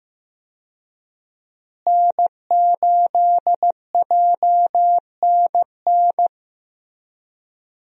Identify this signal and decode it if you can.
Morse code, heard as a keyed tone, about 15 wpm, 700 Hz. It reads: N8JNN
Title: USA Call Signs - 5 Characters - Repeated 3x - 15wpm